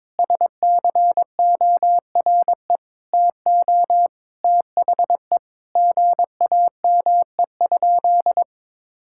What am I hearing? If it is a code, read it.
Morse code, heard as a keyed tone, about 22 wpm, 700 Hz. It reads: SCORE TO THE GAME?
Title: Sentences from Top 1600 Words 22wpm